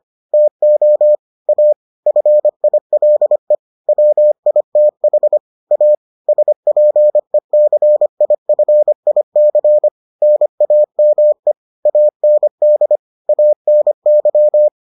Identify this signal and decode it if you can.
Morse code, heard as a keyed tone, about 25 wpm, 600 Hz. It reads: TO A FILE WITH A SPECIFIC NAME AND ANY